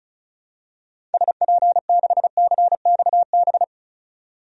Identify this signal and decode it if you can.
Morse code, heard as a keyed tone, about 36 wpm, 700 Hz. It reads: SP6CXB
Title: DXCC-100 Call Signs - 6 Characters - 35wpm